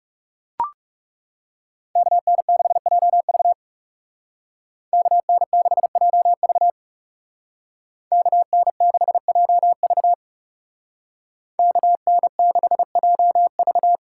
Morse code, heard as a keyed tone, about 36 wpm, 700 Hz. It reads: KN6JV KN6JV KN6JV KN6JV
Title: Speed Racing - USA Call Signs 30wpm